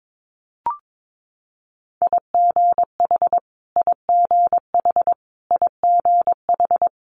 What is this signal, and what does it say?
Morse code, heard as a keyed tone, about 22 wpm, 700 Hz. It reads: IGH IGH IGH
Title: Multi-letter Phonemes Repeated 3x 22wpm